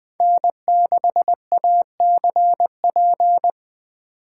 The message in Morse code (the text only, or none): N6ACP